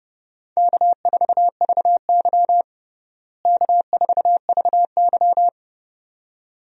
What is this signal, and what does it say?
Morse code, heard as a keyed tone, about 30 wpm, 700 Hz. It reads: K4VY K4VY
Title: USA Call Signs - 4 Characters - Repeated 2x - 30wpm